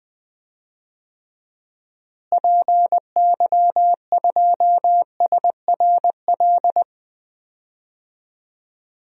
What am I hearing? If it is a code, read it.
Morse code, heard as a keyed tone, about 20 wpm, 700 Hz. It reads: PY2SRL
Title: DXCC-100 Call Signs - 6 Characters - Repeated 2x - 20wpm